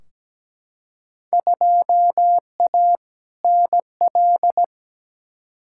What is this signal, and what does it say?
Morse code, heard as a keyed tone, about 17 wpm, 700 Hz. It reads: 2A NL
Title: Field Day Exchanges Encore - 17wpm